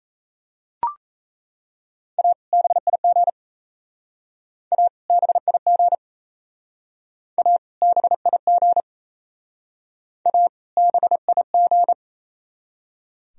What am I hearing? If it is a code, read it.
Morse code, heard as a keyed tone, about 37 wpm, 700 Hz. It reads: A BIG A BIG A BIG A BIG